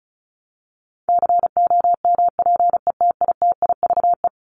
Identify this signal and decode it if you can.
Morse code, heard as a keyed tone, about 34 wpm, 700 Hz. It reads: COMPETITIVE
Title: Top 1001-1250 Words 35wpm